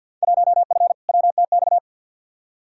Morse code, heard as a keyed tone, about 53 wpm, 700 Hz. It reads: 1F WTX
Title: Field Day Exchanges Encore - 50wpm